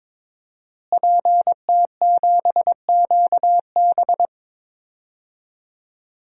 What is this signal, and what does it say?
Morse code, heard as a keyed tone, about 22 wpm, 700 Hz. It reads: PT7QB
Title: DXCC-100 Call Signs - 5 Characters - Repeated 3x - 22wpm